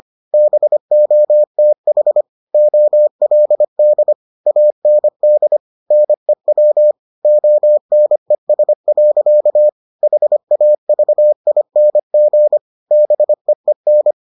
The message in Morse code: BOTH OLD AND NEW ONES. HAVING BEEN

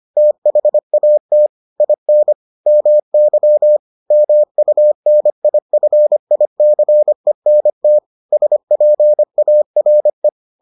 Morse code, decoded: THAT IN MY MUNIFICENT SPARE

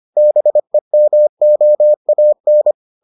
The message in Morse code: BEMOAN

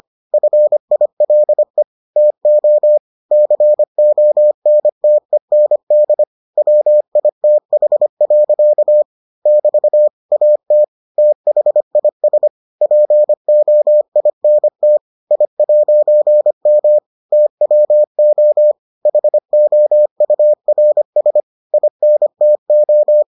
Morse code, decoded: FILE TO CONTEND WITH. <BT> AT THIS POINT I'M TWO HOURS INTO

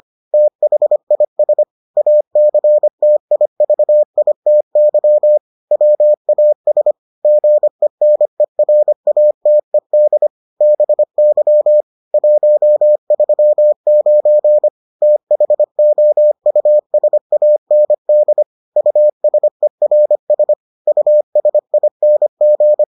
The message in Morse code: THIS ACTIVITY WAS GENERATED BY 139 THOUSAND USERS USING

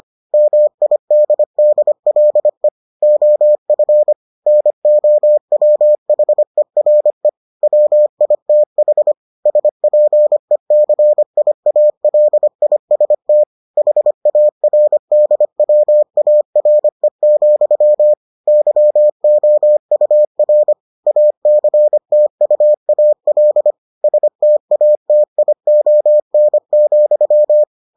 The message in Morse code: MIDDLE OF NOWHERE WITH SPECIALIST HARDWARE, YOUR ACTUAL STATION,

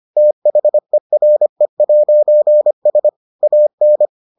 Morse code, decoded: THERE'S AN

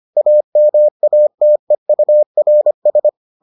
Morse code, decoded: AMATEURS

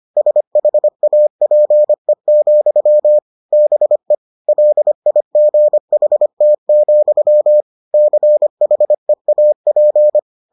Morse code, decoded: SHAPE, BE LIGHT, CHEAP

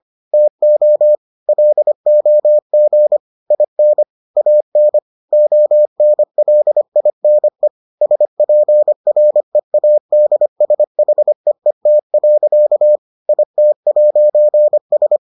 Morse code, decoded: TO LOG IN AN ONLINE SPREADSHEET. IT'S

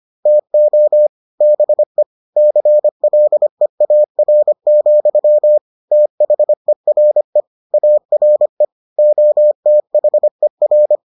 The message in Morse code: TO BE CLEAR, THERE ARE OTHER

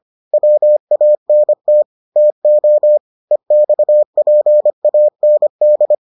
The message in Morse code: WANT TO EXPAND

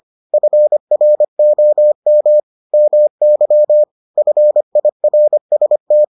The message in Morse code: FROM MY FIRST

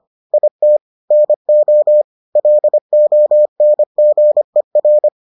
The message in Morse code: IT NO LONGER

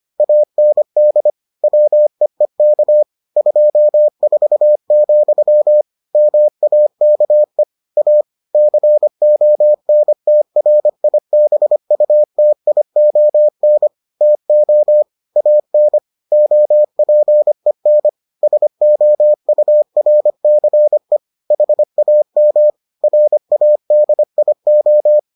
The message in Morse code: AND WEEK 24, MAKE A CONTRIBUTION TO AN OPEN SOURCE HAM RADIO

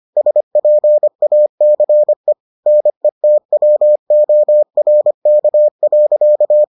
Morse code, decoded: SPACE NETWORK.

